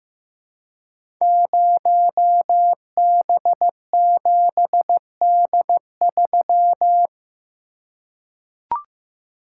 Morse code, heard as a keyed tone, about 15 wpm, 700 Hz. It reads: 0B7D3
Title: Code Groups - 5 Letter:Numbers - 15wpm